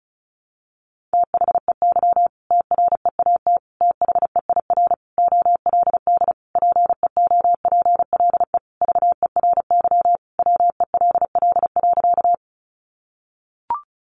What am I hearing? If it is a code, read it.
Morse code, heard as a keyed tone, about 34 wpm, 700 Hz. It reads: THEY TREAT THEIR OLD PEOPLE VERY WELL.
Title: Sentences from Top 1300 Words 35wpm